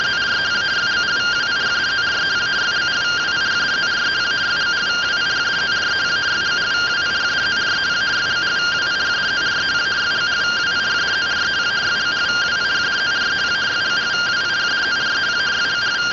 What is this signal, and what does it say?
Two-tone teleprinter signal, about 170 Hz shift.